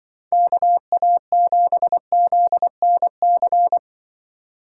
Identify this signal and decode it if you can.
Morse code, heard as a keyed tone, about 24 wpm, 700 Hz. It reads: KA7ZNC
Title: Speed Racing - USA 2x3 Call Signs 20wpm